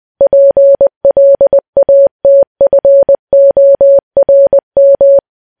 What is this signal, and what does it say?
Morse code, heard as a keyed tone, about 20 wpm, 560 Hz. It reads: PLATFORM
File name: morse